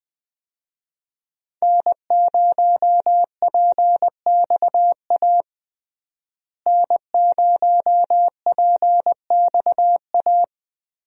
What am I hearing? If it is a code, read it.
Morse code, heard as a keyed tone, about 20 wpm, 700 Hz. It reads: N0PXA N0PXA
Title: USA Call Signs - 5 Characters - Repeated 3x - 20wpm